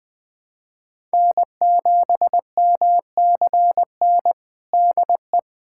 Morse code, heard as a keyed tone, about 20 wpm, 700 Hz. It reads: N7MCN DE
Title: USA Callsign DE Callsign - 5 Characters - 20wpm